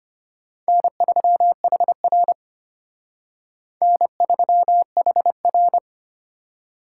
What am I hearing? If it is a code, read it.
Morse code, heard as a keyed tone, about 29 wpm, 700 Hz. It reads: N3HR N3HR
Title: Speed Racing - USA 1x2 Call Signs 25wpm